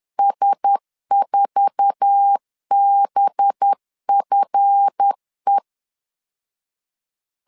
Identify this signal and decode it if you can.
Morse code, heard as a keyed tone, about 10 wpm, 800 Hz. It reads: S4BFE